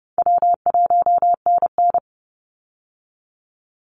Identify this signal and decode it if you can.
Morse code, heard as a keyed tone, about 30 wpm, 700 Hz. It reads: W1NN
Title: Call Signs Repeated 3x - 30wpm